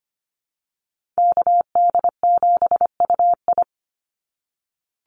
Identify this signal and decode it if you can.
Morse code, heard as a keyed tone, about 25 wpm, 700 Hz. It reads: KD7UI
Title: Call Signs - 25wpm